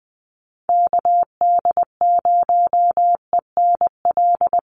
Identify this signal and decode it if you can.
Morse code, heard as a keyed tone, about 20 wpm, 700 Hz. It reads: KD0ENL
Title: USA Call Signs Repeated 3x 20wpm